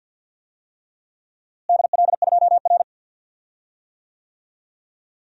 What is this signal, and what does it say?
Morse code, heard as a keyed tone, about 53 wpm, 700 Hz. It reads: DL2R
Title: DXCC-100 Call Signs - 4 Characters - Repeated 3x - 50wpm